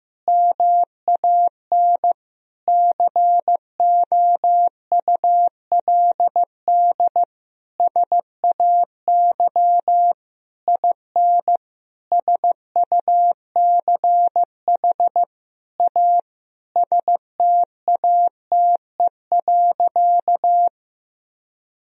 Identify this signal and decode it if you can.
Morse code, heard as a keyed tone, about 16 wpm, 700 Hz. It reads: MAN COULD SAY IN SUCH A STATE.
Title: Sentences from Top 1000 Words - Review Part 2 - 15wpm